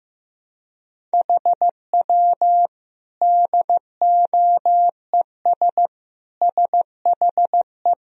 Morse code, heard as a keyed tone, about 15 wpm, 700 Hz. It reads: HW DOES SHE
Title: Sentences from Top 200 Words and Abbreviations 15wpm